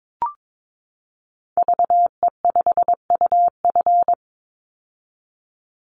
Morse code, heard as a keyed tone, about 22 wpm, 700 Hz. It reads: VE5UF
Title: Call Signs - 22wpm